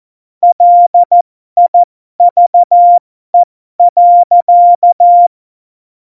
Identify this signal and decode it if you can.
Morse code, heard as a keyed tone, about 14 wpm, spaced as for 12 wpm, 700 Hz. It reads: LIVE.
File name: Morsekurs / Witze